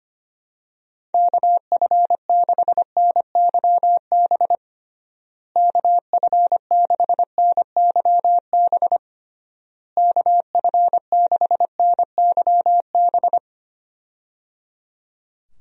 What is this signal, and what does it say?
Morse code, heard as a keyed tone, about 25 wpm, 700 Hz. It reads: KF6NYB KF6NYB KF6NYB
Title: USA Call Signs - 6 Characters - Repeated 3x - 25wpm